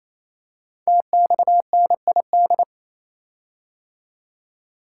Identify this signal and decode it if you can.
Morse code, heard as a keyed tone, about 29 wpm, 700 Hz. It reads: TXNID
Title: Code Groups - 5 Letter:Numbers - 28wpm